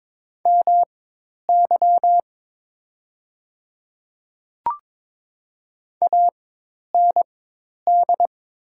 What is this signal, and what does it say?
Morse code, heard as a keyed tone, about 21 wpm, spaced as for 10 wpm, 700 Hz. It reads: MY AND
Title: 4x Character Spacing - Top 100 Words - 22wpm